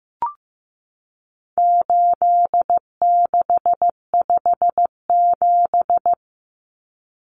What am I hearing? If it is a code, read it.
Morse code, heard as a keyed tone, about 15 wpm, 700 Hz. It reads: 8657
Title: Common Cut Numbers - 4-Digit 15wpm